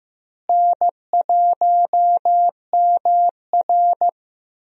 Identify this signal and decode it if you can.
Morse code, heard as a keyed tone, about 15 wpm, 700 Hz. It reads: N1MR
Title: USA Call Signs - 4 Characters - Repeated 2x - 15wpm